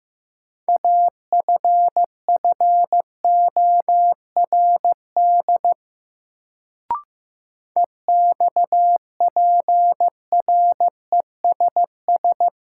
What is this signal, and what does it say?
Morse code, heard as a keyed tone, about 15 wpm, 700 Hz. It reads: AFFORD EXPRESS
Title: Sentences from Top 1100 Words - New Word Drill - 15wpm